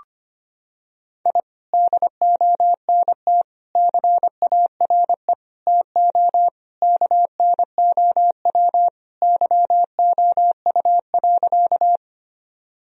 Morse code, heard as a keyed tone, about 25 wpm, 700 Hz. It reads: I DONT CARE TO KNOW YOU.
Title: Sentences from Top 1000 Words - Review Part 1 - 25wpm